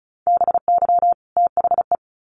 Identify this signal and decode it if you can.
Morse code, heard as a keyed tone, about 34 wpm, 700 Hz. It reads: BY THE
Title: Sets of 2 Words 35wpm